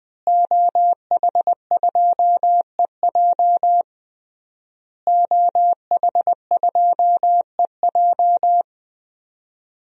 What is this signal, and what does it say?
Morse code, heard as a keyed tone, about 20 wpm, 700 Hz. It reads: OH2EJ OH2EJ
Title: DXCC-100 Call Signs - 5 Characters - Repeated 3x - 20wpm